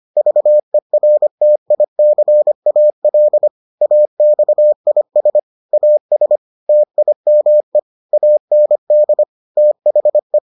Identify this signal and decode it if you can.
Morse code, heard as a keyed tone, about 25 wpm, 600 Hz. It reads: VERTICAL AXIS AS TIME AND THE